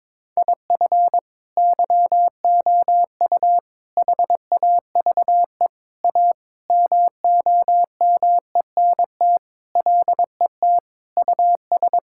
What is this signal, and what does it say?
Morse code, heard as a keyed tone, about 22 wpm, 700 Hz. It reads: IF YOU HAVE A MOMENT LET US
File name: practice / 22wpm / Sentences with 4-Word Phrases - 22wpm